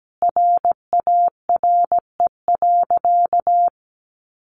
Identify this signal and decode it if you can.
Morse code, heard as a keyed tone, about 17 wpm, 700 Hz. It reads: RARE.
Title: Sentences from Top 1100 Words 17wpm